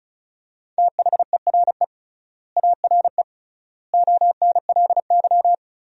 Morse code, heard as a keyed tone, about 36 wpm, 700 Hz. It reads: THERE ARE ONLY
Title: 3x Word-Spacing - Sentences from Top 200 Words - 35wpm